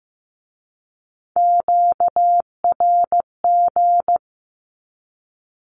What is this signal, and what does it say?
Morse code, heard as a keyed tone, about 15 wpm, 700 Hz. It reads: QRG
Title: Q-Codes with Spoken Meaning - 15wpm